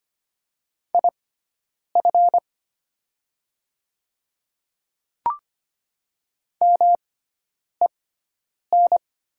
Morse code, heard as a keyed tone, about 25 wpm, spaced as for 9 wpm, 700 Hz. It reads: IF MEN
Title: 6x Character Spacing - Top 200 Words - 25wpm